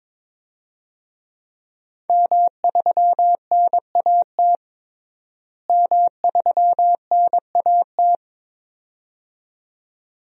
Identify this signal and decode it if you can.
Morse code, heard as a keyed tone, about 22 wpm, 700 Hz. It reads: M3NAT M3NAT
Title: DXCC-100 Call Signs - 5 Characters - Repeated 2x - 22wpm